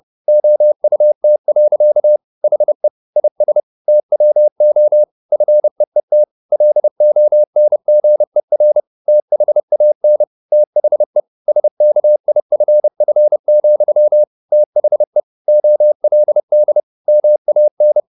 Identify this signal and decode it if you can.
Morse code, heard as a keyed tone, about 30 wpm, 600 Hz. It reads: OUT. HE IS TWO FEET LONGER THAN THE SKIFF, THE OLD MAN
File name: the old man and the sea / the old man and the sea 1 30wpm